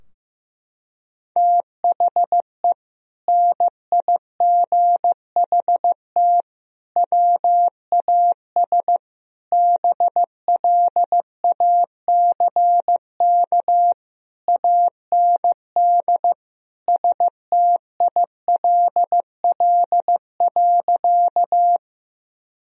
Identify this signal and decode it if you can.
Morse code, heard as a keyed tone, about 15 wpm, 700 Hz. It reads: THE NIGHT WAS BLACK AND STILL.
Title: Sentences from Top 1000 Words - Review Part 2 - 15wpm